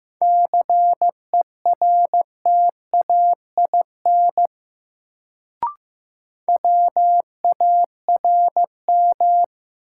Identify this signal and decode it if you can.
Morse code, heard as a keyed tone, about 15 wpm, 700 Hz. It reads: CERTAIN WARM
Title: Sentences from Top 500 Words - New Word Drill - 15wpm